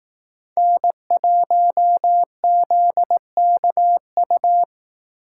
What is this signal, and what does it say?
Morse code, heard as a keyed tone, about 18 wpm, 700 Hz. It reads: N1ZKU
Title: Speed Racing - USA 1x3 Call Signs 15wpm